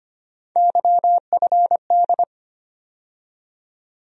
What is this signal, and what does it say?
Morse code, heard as a keyed tone, about 25 wpm, 700 Hz. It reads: YFD
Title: USA Call Sign Postfixes - 3 Letters 25wpm